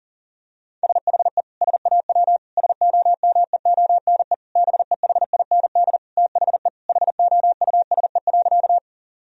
Morse code, heard as a keyed tone, about 40 wpm, 690 Hz. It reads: SHE SAW SOMEONE BEHIND THE HOUSE.
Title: Sentences from Top 1000 Words - Review Part 4 - 40wpm